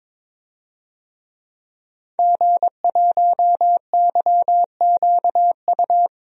Morse code, heard as a keyed tone, about 22 wpm, 700 Hz. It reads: G1YQU
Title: DXCC-100 Call Signs - 5 Characters - Repeated 2x - 22wpm